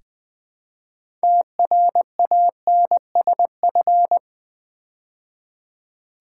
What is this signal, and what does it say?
Morse code, heard as a keyed tone, about 20 wpm, 700 Hz. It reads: TRANSF
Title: Build A Word - 2 Letters At A Time - 8-Letter Words - 20wpm